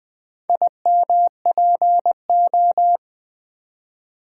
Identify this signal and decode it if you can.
Morse code, heard as a keyed tone, about 20 wpm, 700 Hz. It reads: IMPO
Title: Build A Word - 2 Letters At A Time - 6-Letter Words - 20wpm